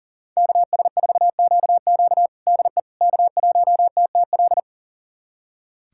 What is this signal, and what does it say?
Morse code, heard as a keyed tone, about 40 wpm, 700 Hz. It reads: KS4QQ DE K1TTL
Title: USA Callsign DE Callsign - 5 Characters - 40wpm